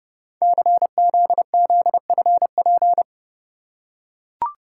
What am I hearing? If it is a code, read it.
Morse code, heard as a keyed tone, about 30 wpm, 700 Hz. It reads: CZZFP
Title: Code Groups - 5 Letter:Numbers - 30wpm